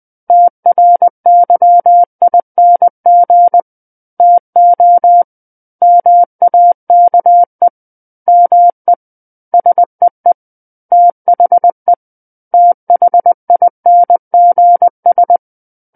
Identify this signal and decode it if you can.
Morse code, heard as a keyed tone, about 20 wpm, 700 Hz. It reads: TRYING TO MAKE ME SEE THE THINGS
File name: 20 WPM